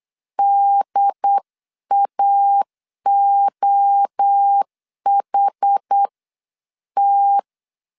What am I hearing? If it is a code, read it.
Morse code, heard as a keyed tone, about 8 wpm, spaced as for 7 wpm, 800 Hz. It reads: DAOHT